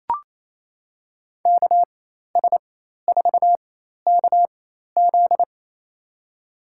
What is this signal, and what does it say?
Morse code, heard as a keyed tone, about 28 wpm, spaced as for 13 wpm, 700 Hz. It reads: KS4KZ
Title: USA Call Signs - 5 Characters - 4x Character Spacing - 28wpm